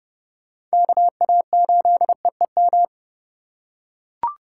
Morse code, heard as a keyed tone, about 29 wpm, 700 Hz. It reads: KA8EEM